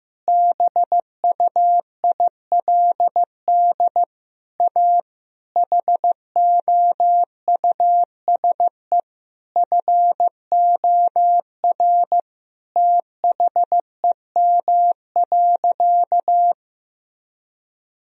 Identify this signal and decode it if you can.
Morse code, heard as a keyed tone, about 15 wpm, 700 Hz. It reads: BUILD A HOUSE FOR THEM.
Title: Sentences from Top 1000 Words - Review Part 1 - 15wpm